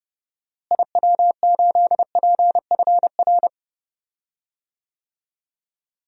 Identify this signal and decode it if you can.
Morse code, heard as a keyed tone, about 30 wpm, 700 Hz. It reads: IW8PFR
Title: DXCC-100 Call Signs - 6 Characters - Repeated 3x - 30wpm